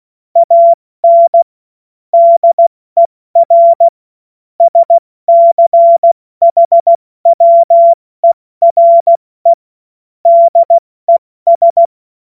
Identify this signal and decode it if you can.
Morse code, heard as a keyed tone, about 16 wpm, 680 Hz. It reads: AN DER SCHWERE DES
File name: Morsekurs / Witze / Witz 24 Juni Mai 2025 70 F80